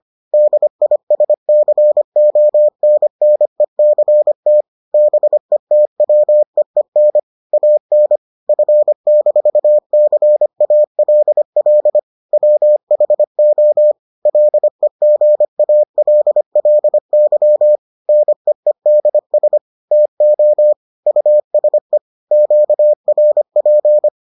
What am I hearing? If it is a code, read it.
Morse code, heard as a keyed tone, about 25 wpm, 600 Hz. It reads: DISCONNECT BETWEEN AN F-CALL WHO LEGALLY NEEDS TO USE QRP